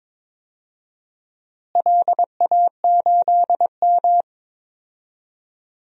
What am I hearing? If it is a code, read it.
Morse code, heard as a keyed tone, about 22 wpm, 700 Hz. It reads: LA8M
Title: DXCC-100 Call Signs - 4 Characters - Repeated 2x - 22wpm